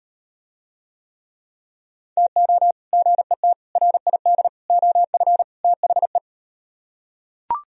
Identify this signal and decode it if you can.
Morse code, heard as a keyed tone, about 38 wpm, 700 Hz. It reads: TO GET RID OF THE